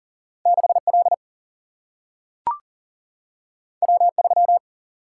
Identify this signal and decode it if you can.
Morse code, heard as a keyed tone, about 42 wpm, 700 Hz. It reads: 6 L W 3
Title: Single-Letter Single-Number 40wpm